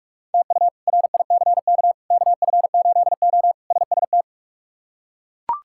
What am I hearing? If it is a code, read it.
Morse code, heard as a keyed tone, about 42 wpm, 700 Hz. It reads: TU RICK KF8O SST
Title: K1USN SST QSO Practice - 45wpm